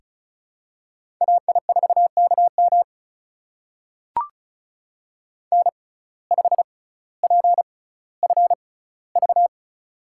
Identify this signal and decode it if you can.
Morse code, heard as a keyed tone, about 36 wpm, spaced as for 18 wpm, 700 Hz. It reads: AI4KM N5PFV
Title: USA Call Signs - 5 Characters - 6x Character Spacing - 35wpm